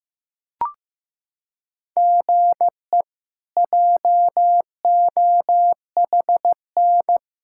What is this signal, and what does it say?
Morse code, heard as a keyed tone, about 15 wpm, 700 Hz. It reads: GE JOHN
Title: K1USN SST QSO Practice - 15wpm